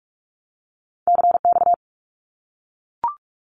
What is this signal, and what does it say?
Morse code, heard as a keyed tone, about 44 wpm, 700 Hz. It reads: CX
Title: Abbreviations 45wpm